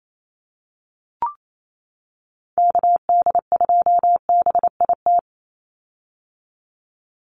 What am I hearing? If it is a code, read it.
Morse code, heard as a keyed tone, about 28 wpm, 700 Hz. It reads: KD2BIT
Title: USA Call Signs 28wpm